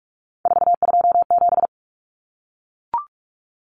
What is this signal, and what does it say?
Morse code, heard as a keyed tone, about 44 wpm, 700 Hz. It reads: 427
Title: Numbers - 3-Digit 45wpm